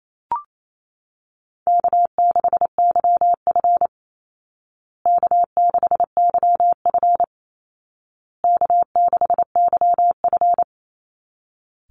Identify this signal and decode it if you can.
Morse code, heard as a keyed tone, about 28 wpm, 700 Hz. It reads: K6YF K6YF K6YF
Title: Easy Call Signs Repeated 3x - 28wpm